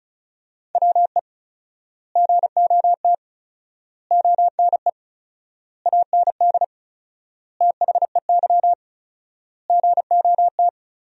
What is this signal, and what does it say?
Morse code, heard as a keyed tone, about 36 wpm, 700 Hz. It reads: WE GOT ONE AND THEY GOT
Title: 4x Word-Spacing - Sentences from Top 300 Words - 35wpm